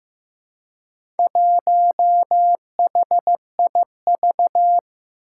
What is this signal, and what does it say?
Morse code, heard as a keyed tone, about 15 wpm, 700 Hz. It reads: 1HIV
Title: Four Challenging Letter-Number Combinations 15wpm